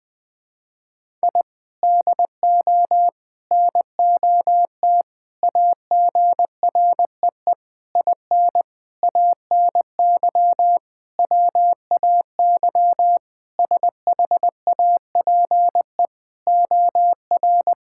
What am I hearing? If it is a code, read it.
Morse code, heard as a keyed tone, about 20 wpm, 700 Hz. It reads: I DO NOT AGREE IN ANY WAY SHAPE OR
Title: Sentences with 6-Word Phrases - 20wpm